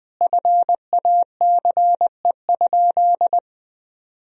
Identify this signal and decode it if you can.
Morse code, heard as a keyed tone, about 20 wpm, 700 Hz. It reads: FACE?
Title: Sentences from Top 1000 Words - Review Part 2 - 20wpm